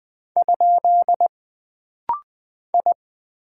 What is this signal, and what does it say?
Morse code, heard as a keyed tone, about 20 wpm, 700 Hz. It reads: ? I
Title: Single Character (Letter:Number:Punctuation:Prosign) - Rapid-Fire - 20wpm